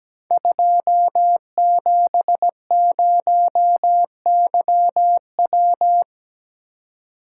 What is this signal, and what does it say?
Morse code, heard as a keyed tone, about 17 wpm, 700 Hz. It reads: 270YW
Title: Code Groups - 5 Letter:Numbers - 17wpm